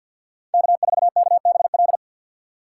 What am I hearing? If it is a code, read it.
Morse code, heard as a keyed tone, about 53 wpm, 700 Hz. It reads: K4KBL
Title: K1USN SST QSO Practice - 50wpm